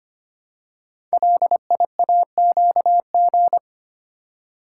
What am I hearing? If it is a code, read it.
Morse code, heard as a keyed tone, about 25 wpm, 700 Hz. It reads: LIAQG
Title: Code Groups - 5 Letters - 25wpm